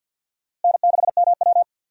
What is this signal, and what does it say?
Morse code, heard as a keyed tone, about 46 wpm, 700 Hz. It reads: N6MW
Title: USA Call Signs - 4 Characters - Repeated 3x - 50wpm